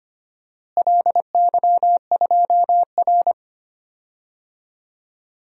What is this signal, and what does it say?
Morse code, heard as a keyed tone, about 25 wpm, 700 Hz. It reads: LY2R
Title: DXCC-100 Call Signs - 4 Characters - Repeated 2x - 25wpm